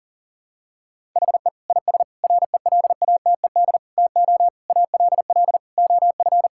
Morse code, heard as a keyed tone, about 40 wpm, 700 Hz. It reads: HE IS RELATED TO ALL OF
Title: Sentences from Top 1000 Words - Review Part 5 - 40wpm